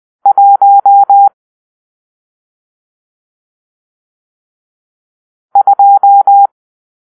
Morse code, heard as a keyed tone, about 20 wpm, 800 Hz. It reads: 1 2
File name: arrl / Grp-6 Drill-1